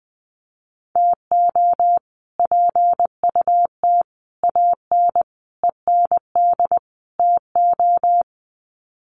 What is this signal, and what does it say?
Morse code, heard as a keyed tone, about 20 wpm, 700 Hz. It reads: TO PUT AN END TO
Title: Sets of 5 Words Encore2 20wpm